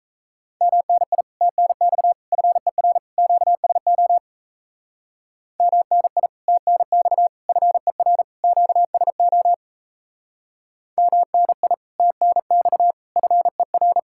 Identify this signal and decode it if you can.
Morse code, heard as a keyed tone, about 36 wpm, 700 Hz. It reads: MNI TNX FER QSO MNI TNX FER QSO MNI TNX FER